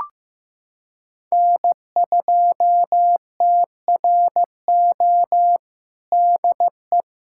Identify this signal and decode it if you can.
Morse code, heard as a keyed tone, about 15 wpm, 700 Hz. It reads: N2TRO DE
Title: USA Callsign DE Callsign - 5 Characters - 15wpm